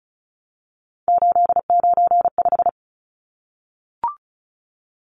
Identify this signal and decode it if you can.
Morse code, heard as a keyed tone, about 34 wpm, 700 Hz. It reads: 895
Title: Numbers - 3-Digit 35wpm